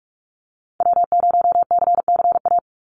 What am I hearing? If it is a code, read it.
Morse code, heard as a keyed tone, about 46 wpm, 700 Hz. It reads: W0CCA
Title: NA SOTA Activator Callsigns 45wpm